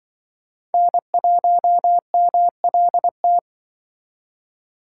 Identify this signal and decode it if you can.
Morse code, heard as a keyed tone, about 24 wpm, 700 Hz. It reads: N1MLT
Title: Speed Racing - USA 1x3 Call Signs 20wpm